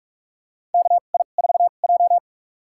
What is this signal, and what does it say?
Morse code, heard as a keyed tone, about 46 wpm, spaced as for 32 wpm, 700 Hz. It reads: KI4J
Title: USA Call Signs - 4 Characters - 2x Character Spacing - 45wpm